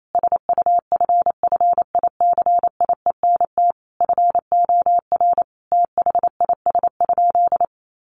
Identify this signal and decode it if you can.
Morse code, heard as a keyed tone, about 28 wpm, 700 Hz. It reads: SUFFICIENT FOR THIS?
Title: Sentences from Top 1400 Words 28wpm